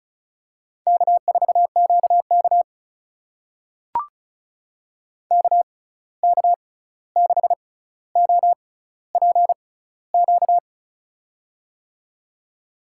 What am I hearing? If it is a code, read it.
Morse code, heard as a keyed tone, about 36 wpm, spaced as for 16 wpm, 700 Hz. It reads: K4QK KK6OPQ
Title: USA Call Signs - 6x Character Spacing - 35wpm